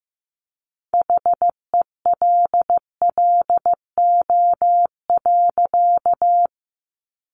Morse code, heard as a keyed tone, about 15 wpm, 700 Hz. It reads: HELLO.
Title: Sentences from Top 1100 Words 15wpm